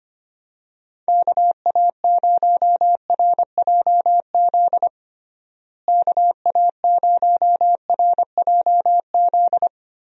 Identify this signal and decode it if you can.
Morse code, heard as a keyed tone, about 25 wpm, 700 Hz. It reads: KA0RJZ KA0RJZ
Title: USA Call Signs - 6 Characters - Repeated 3x - 25wpm